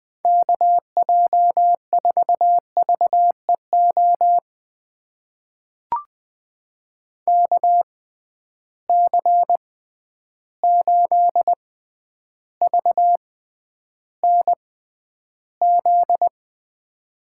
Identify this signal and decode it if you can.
Morse code, heard as a keyed tone, about 20 wpm, spaced as for 10 wpm, 700 Hz. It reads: KJ4VEO KC8VNZ
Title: USA Call Signs - 6 Characters - 6x Character Spacing - 20wpm